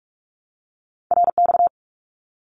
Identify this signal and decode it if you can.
Morse code, heard as a keyed tone, about 44 wpm, 700 Hz. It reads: RX
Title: Abbreviations 45wpm